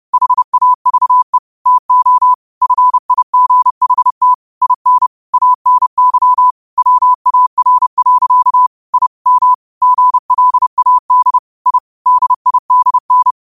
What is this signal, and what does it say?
Morse code, heard as a keyed tone, about 30 wpm, 1000 Hz. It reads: HAVE TO FIGHT IN ANY WAR. I M GLAD I DIDN